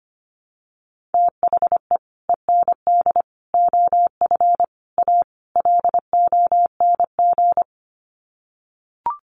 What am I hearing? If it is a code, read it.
Morse code, heard as a keyed tone, about 25 wpm, 700 Hz. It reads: THE END OF A LONG
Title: Sets of 5 Words Encore2 25wpm